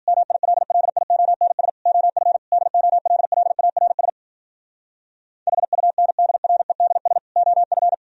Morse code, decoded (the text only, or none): MILLIONS OF DOLLARS HUNDREDS OF